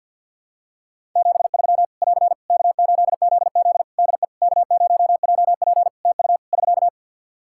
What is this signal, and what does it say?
Morse code, heard as a keyed tone, about 51 wpm, 700 Hz. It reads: 73 <AR> K8ZZ DE K0JP TU <SK>
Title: Practice QSO - Exchange 4 - Ending - 50wpm